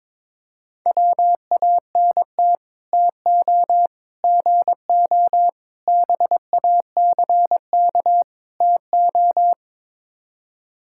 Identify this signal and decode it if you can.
Morse code, heard as a keyed tone, about 22 wpm, 700 Hz. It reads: WANT TO GO BACK TO
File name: practice / 22wpm / Build A Phrase - 5-Word Phrases - 22wpm